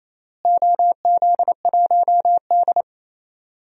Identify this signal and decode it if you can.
Morse code, heard as a keyed tone, about 28 wpm, 700 Hz. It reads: OZ1D
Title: DXCC-100 Call Signs - 4 Characters - Repeated 2x - 28wpm